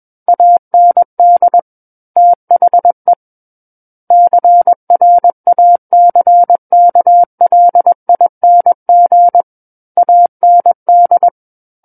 Morse code, heard as a keyed tone, about 21 wpm, 700 Hz. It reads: AND THE CRACKLING AND
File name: Book One / 12-Chapter 12 - 21 WPM